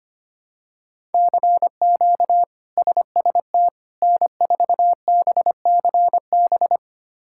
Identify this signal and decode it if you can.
Morse code, heard as a keyed tone, about 25 wpm, 700 Hz. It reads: CQ SST N4BCB
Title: K1USN SST QSO Practice - 25wpm